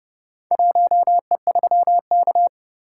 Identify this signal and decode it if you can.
Morse code, heard as a keyed tone, about 29 wpm, 700 Hz. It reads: 1E3K